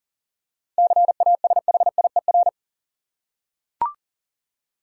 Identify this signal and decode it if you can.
Morse code, heard as a keyed tone, about 40 wpm, 700 Hz. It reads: CASHIER
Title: Practice QSO - Exchange 3 - More Info - 40wpm